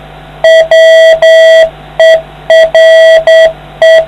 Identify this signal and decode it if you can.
Morse code, heard as a keyed tone, about 9 wpm, 650 Hz. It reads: WERE